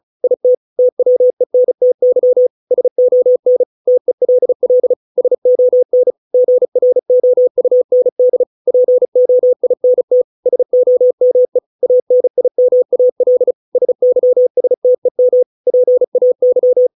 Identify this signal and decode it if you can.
Morse code, heard as a keyed tone, about 34 wpm, 500 Hz. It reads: IT TWENTY SON TELL SON GROUND POINT SOME ANIMAL SYSTEM PAY